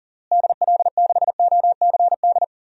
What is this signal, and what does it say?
Morse code, heard as a keyed tone, about 40 wpm, 700 Hz. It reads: DL6OCD
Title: DXCC-100 Call Signs - 6 Characters - Repeated 3x - 40wpm